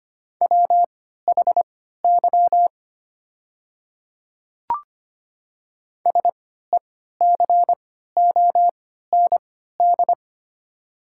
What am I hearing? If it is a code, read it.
Morse code, heard as a keyed tone, about 25 wpm, spaced as for 14 wpm, 700 Hz. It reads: WHY SECOND
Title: 3x Character Spacing - Top 500 Words - 25wpm